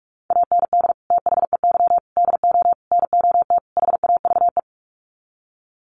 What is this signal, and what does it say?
Morse code, heard as a keyed tone, about 44 wpm, 700 Hz. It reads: AND THEY DO NOT HAVE
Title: Sets of 5 Words Encore2 45wpm